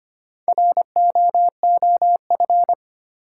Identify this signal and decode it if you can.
Morse code, heard as a keyed tone, about 25 wpm, 700 Hz. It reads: ROOF
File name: practice / 25wpm / Easy Words with Hints - 25wpm